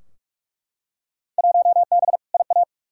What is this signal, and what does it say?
Morse code, heard as a keyed tone, about 41 wpm, 700 Hz. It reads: 1B IA
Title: Field Day Exchanges Encore - 45wpm